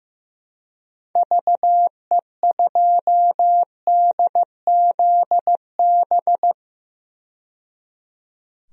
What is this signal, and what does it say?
Morse code, heard as a keyed tone, about 15 wpm, 700 Hz. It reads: VE2DZB
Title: DXCC-100 Call Signs - 6 Characters - 15wpm